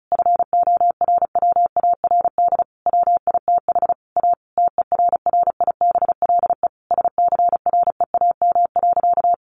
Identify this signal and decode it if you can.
Morse code, heard as a keyed tone, about 34 wpm, 700 Hz. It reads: FORWARD WITH A TERRIBLE SCREAM.
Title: A Sound Of Thunder 35wpm